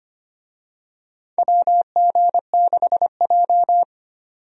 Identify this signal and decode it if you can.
Morse code, heard as a keyed tone, about 25 wpm, 700 Hz. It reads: WG6J
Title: USA Call Signs - 4 Characters - 25wpm